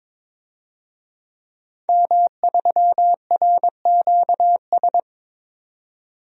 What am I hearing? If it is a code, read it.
Morse code, heard as a keyed tone, about 22 wpm, 700 Hz. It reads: M3RQS
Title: DXCC-100 Call Signs - 5 Characters - 22wpm